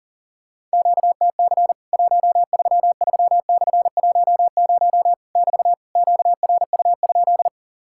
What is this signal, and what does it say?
Morse code, heard as a keyed tone, about 40 wpm, 700 Hz. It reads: QTC 133/10 <BT> QRU?
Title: Worked All Europe DX Contest - QTC Exchange - 40wpm